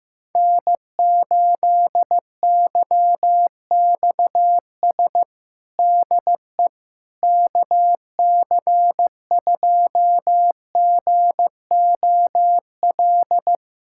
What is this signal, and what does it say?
Morse code, heard as a keyed tone, about 15 wpm, 700 Hz. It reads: N8YXS DE KC2GOL
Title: USA Callsign DE Callsign 15wpm